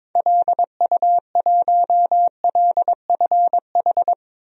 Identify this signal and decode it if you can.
Morse code, heard as a keyed tone, about 22 wpm, 700 Hz. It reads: LU1LFH
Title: DXCC-100 Call Signs - 6 Characters - Repeated 3x - 22wpm